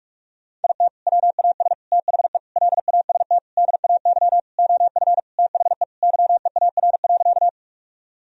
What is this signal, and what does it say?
Morse code, heard as a keyed tone, about 42 wpm, 700 Hz. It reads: IT WAS THE LAST DAY OF THE YEAR.
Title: Sentences from Top 1000 Words - Review Part 1 - 45wpm